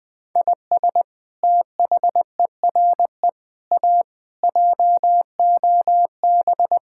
Morse code, heard as a keyed tone, about 20 wpm, 700 Hz. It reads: IS THERE A JOB
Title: Sentences from Top 1600 Words 20wpm